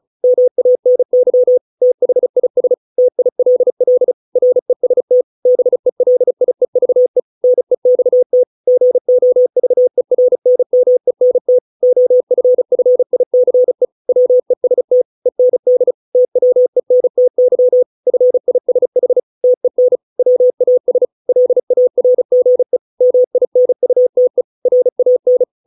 Morse code, heard as a keyed tone, about 34 wpm, 500 Hz. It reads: MANY THIS TILL REST BELIEVE NEXT GOVERNMENT OFFICE WEST END TWENTY FISH TEN WAS LARGE MINUTE RAN